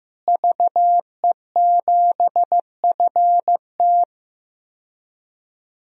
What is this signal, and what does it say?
Morse code, heard as a keyed tone, about 15 wpm, 700 Hz. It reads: VE7FT
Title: DXCC-100 Call Signs - 5 Characters - Repeated 2x - 15wpm